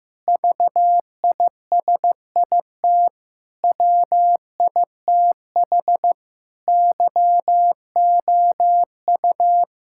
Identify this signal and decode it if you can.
Morse code, heard as a keyed tone, about 15 wpm, 700 Hz. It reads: VISIT WITH YOU
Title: Sentences from Top 1000 Words - Review Part 6 - 15wpm